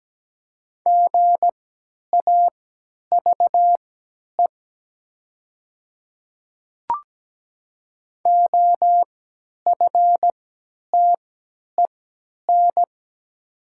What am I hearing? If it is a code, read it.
Morse code, heard as a keyed tone, about 17 wpm, spaced as for 10 wpm, 690 Hz. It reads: GAVE OFTEN
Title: 3x Character Spacing - Top 500 Words - 17wpm